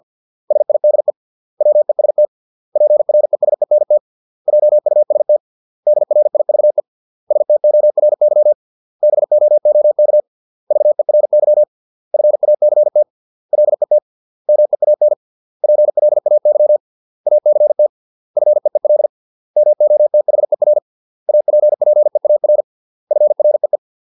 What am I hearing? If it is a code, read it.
Morse code, heard as a keyed tone, about 50 wpm, 600 Hz. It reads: SIDE WEST PRESENT JUST DRIVE STORY BOOK VERY FACT LET MEAN PLAY ACT FEEL MOTHER APPEAR FREE